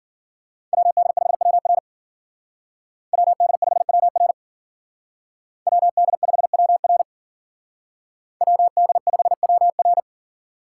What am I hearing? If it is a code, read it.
Morse code, heard as a keyed tone, about 55 wpm, 700 Hz. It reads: WD5WR WD5WR WD5WR WD5WR